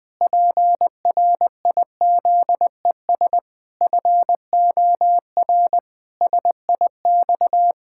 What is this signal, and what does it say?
Morse code, heard as a keyed tone, about 20 wpm, 700 Hz. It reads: PRIZES FOR SIX
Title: Pangram Challenge v2 - 20wpm